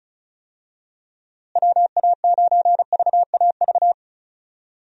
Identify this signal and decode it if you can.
Morse code, heard as a keyed tone, about 36 wpm, 700 Hz. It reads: WA9VAV
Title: USA Call Signs - 6 Characters - Repeated 3x - 35wpm